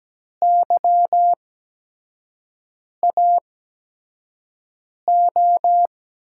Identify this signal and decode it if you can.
Morse code, heard as a keyed tone, about 17 wpm, spaced as for 5 wpm, 700 Hz. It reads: YAO
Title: USA Call Signs - 5 Characters - 8x Character Spacing - 17wpm